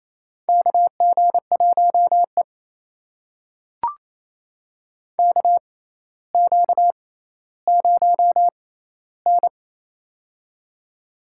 Morse code, heard as a keyed tone, about 28 wpm, spaced as for 14 wpm, 700 Hz. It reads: KG1E KQ0N
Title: USA Call Signs - 4 Characters - 6x Character Spacing - 28wpm